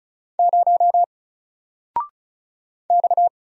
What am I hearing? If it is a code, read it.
Morse code, heard as a keyed tone, about 36 wpm, 700 Hz. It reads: 0 X
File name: Single-Letter-Number-35wpm